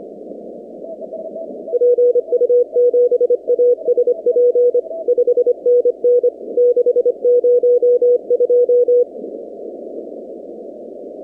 Morse code, heard as a keyed tone, about 25 wpm, 490 Hz. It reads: PU7ASP 5NN 602
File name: 2019-cw-wpx-kt5j-k5pi